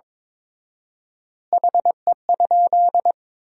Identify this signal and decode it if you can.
Morse code, heard as a keyed tone, about 22 wpm, 700 Hz. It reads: HE?
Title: 4x Word-Spacing - Sentences from Top 100 Words - 22wpm